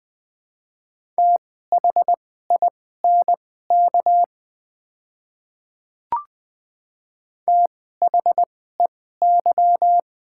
Morse code, heard as a keyed tone, about 20 wpm, spaced as for 15 wpm, 700 Hz. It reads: THINK THEY
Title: 2x Character Spacing - Top 200 Words - 20wpm